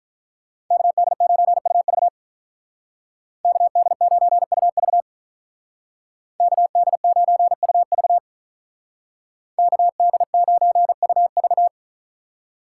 Morse code, heard as a keyed tone, about 43 wpm, 700 Hz. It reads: KD9UV KD9UV KD9UV KD9UV
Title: Speed Racing - USA 2x2 Call Signs 35wpm